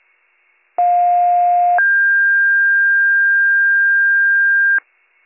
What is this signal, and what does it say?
Two-tone page, heard as a one-second tone, then a three-second tone, about 710 then 1600 Hz. It is heard from 0.8 to 4.8 s.